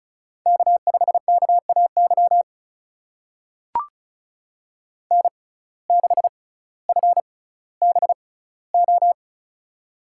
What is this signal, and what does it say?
Morse code, heard as a keyed tone, about 36 wpm, spaced as for 18 wpm, 700 Hz. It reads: K5KAY N6FBO
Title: USA Call Signs - 5 Characters - 6x Character Spacing - 35wpm